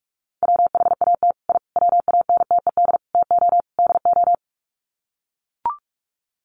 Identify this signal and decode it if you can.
Morse code, heard as a keyed tone, about 44 wpm, 700 Hz. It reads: WHAT I WANTED TO DO
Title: Sets of 5 Words Encore2 45wpm